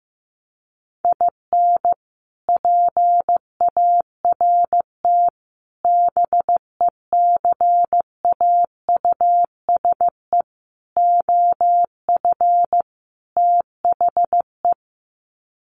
Morse code, heard as a keyed tone, about 15 wpm, 700 Hz. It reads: IN PART BECAUSE OF THE
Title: Sets of 5 Words Encore2 15wpm